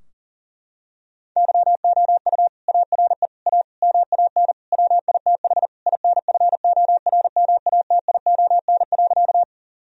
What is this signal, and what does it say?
Morse code, heard as a keyed tone, about 40 wpm, 700 Hz. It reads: YOU ARE A MAN WITH INFORMATION.
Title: Sentences from Top 1000 Words - Review Part 3 - 40wpm